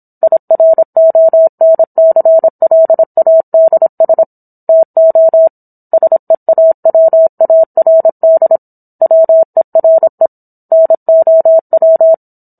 Morse code, heard as a keyed tone, about 27 wpm, 650 Hz. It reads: IRONCLADS TO SEAWARD WERE NOW
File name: Book One / 17-Chapter 17 - 26 WPM